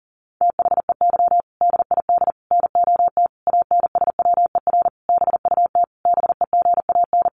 Morse code, heard as a keyed tone, about 40 wpm, 700 Hz. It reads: THEY DID NOT ANSWER BUT BEGAN
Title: Sentences from Top 600 Words 40wpm